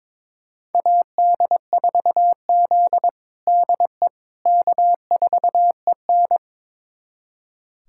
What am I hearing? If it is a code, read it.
Morse code, heard as a keyed tone, about 22 wpm, 700 Hz. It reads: AD4Z DE K4EN
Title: USA Callsign DE Callsign - 4 Characters - 22wpm